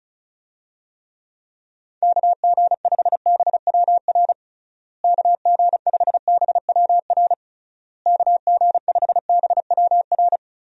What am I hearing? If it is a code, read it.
Morse code, heard as a keyed tone, about 36 wpm, 700 Hz. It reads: KG5BWR KG5BWR KG5BWR
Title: USA Call Signs - 6 Characters - Repeated 3x - 35wpm